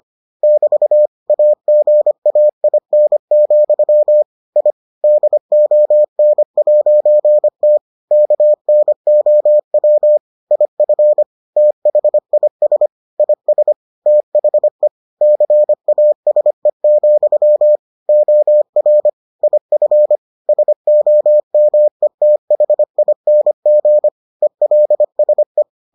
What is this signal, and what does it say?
Morse code, heard as a keyed tone, about 25 wpm, 600 Hz. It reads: <BT> AGAIN, I DON'T KNOW IF THIS IS THE CASE, OR IF SOMETHING ELSE